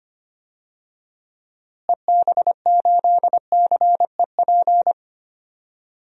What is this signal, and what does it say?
Morse code, heard as a keyed tone, about 25 wpm, 700 Hz. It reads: EB8CEP
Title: DXCC-100 Call Signs - 6 Characters - Repeated 3x - 25wpm